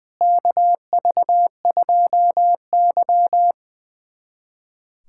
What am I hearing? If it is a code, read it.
Morse code, heard as a keyed tone, about 21 wpm, 700 Hz. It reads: KV2Y
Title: USA Callsign DE Callsign - 4 Characters - 20wpm